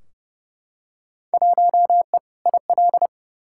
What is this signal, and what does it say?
Morse code, heard as a keyed tone, about 29 wpm, 700 Hz. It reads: 1E IL
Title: Field Day Exchanges - 30wpm